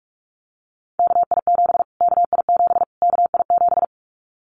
Morse code, heard as a keyed tone, about 44 wpm, 700 Hz. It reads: KI7 KI7 KI7
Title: Callsign Prefixes for Top 25 DXCC - Repeated 3x - 45wpm